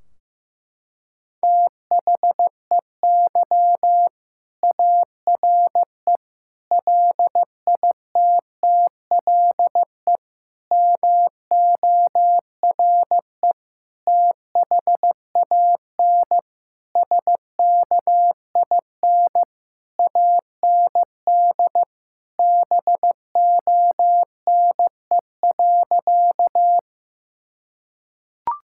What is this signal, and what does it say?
Morse code, heard as a keyed tone, about 16 wpm, 700 Hz. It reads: THEY ARE LITTLE MORE THAN SKIN AND BONE.
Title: Sentences from Top 1600 Words 15wpm